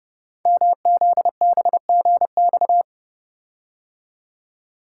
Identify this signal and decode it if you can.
Morse code, heard as a keyed tone, about 30 wpm, 700 Hz. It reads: MZBGX
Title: Code Groups - 5 Letters - 30wpm